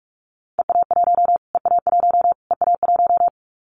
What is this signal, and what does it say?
Morse code, heard as a keyed tone, about 44 wpm, 700 Hz. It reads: EA1 EA1 EA1
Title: Callsign Prefixes for Top 25 DXCC - Repeated 3x - 45wpm